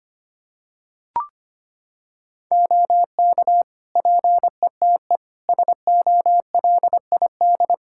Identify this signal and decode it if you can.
Morse code, heard as a keyed tone, about 25 wpm, 700 Hz. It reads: OK PETE SOLID
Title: Practice QSO - Exchange 3 - More Info - 25wpm